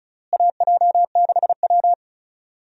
Morse code, heard as a keyed tone, about 34 wpm, 700 Hz. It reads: AJ6W
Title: USA Call Signs - 4 Characters - Repeated 2x - 35wpm